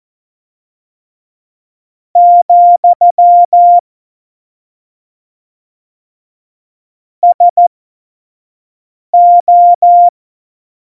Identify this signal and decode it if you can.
Morse code, heard as a keyed tone, about 14 wpm, spaced as for 5 wpm, 700 Hz. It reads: , SO
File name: Morsekurs / Witze / Witz 03.Juli 2018, 25 F70